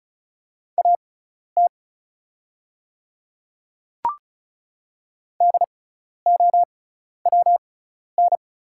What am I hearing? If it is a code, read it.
Morse code, heard as a keyed tone, about 34 wpm, spaced as for 12 wpm, 690 Hz. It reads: AT DOWN
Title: 6x Character Spacing - Top 200 Words - 35wpm